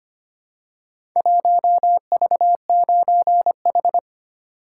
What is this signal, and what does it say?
Morse code, heard as a keyed tone, about 25 wpm, 700 Hz. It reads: 1V9H
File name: practice / 25wpm / Four Letter-Number Combinations - 25wpm